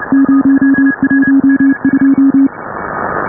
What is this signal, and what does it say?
Morse code, heard as a keyed tone, about 32 wpm, 270 Hz, with noise. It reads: 012